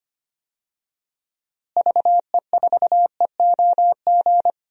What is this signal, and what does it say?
Morse code, heard as a keyed tone, about 25 wpm, 700 Hz. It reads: VE4EOG
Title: DXCC-100 Call Signs - 6 Characters - Repeated 3x - 25wpm